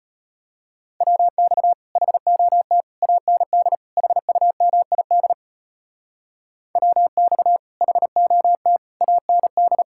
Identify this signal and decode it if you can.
Morse code, heard as a keyed tone, about 32 wpm, 700 Hz. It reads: WX HOT AND HUMID WX HOT AND